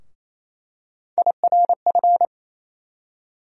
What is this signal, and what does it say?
Morse code, heard as a keyed tone, about 28 wpm, 700 Hz. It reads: IRF
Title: USA Call Sign Postfixes - 3 Letters 28wpm